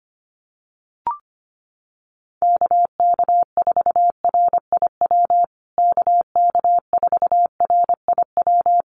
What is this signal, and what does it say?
Morse code, heard as a keyed tone, about 25 wpm, 700 Hz. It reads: KK4RIW KK4RIW
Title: USA Call Signs Repeated 3x 25wpm